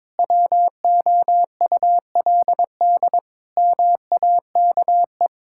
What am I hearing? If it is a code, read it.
Morse code, heard as a keyed tone, about 22 wpm, 700 Hz. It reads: WOULD MAKE
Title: Sentences from Top 1000 Words - Review Part 6 - 22wpm